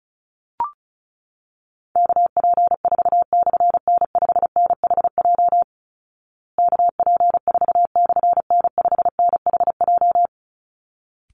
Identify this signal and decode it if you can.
Morse code, heard as a keyed tone, about 34 wpm, 700 Hz. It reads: KP4/N5NHJ KP4/N5NHJ
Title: Call Signs Repeated 3x - 35wpm